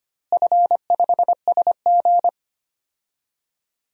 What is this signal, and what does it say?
Morse code, heard as a keyed tone, about 25 wpm, 700 Hz. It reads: F5SG
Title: DXCC-100 Call Signs - 4 Characters - Repeated 3x - 25wpm